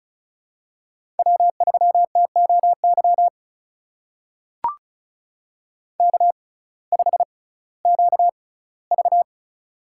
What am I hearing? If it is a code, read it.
Morse code, heard as a keyed tone, about 36 wpm, spaced as for 19 wpm, 700 Hz. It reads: W3TOY K5QV
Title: USA Call Signs - 6x Character Spacing - 35wpm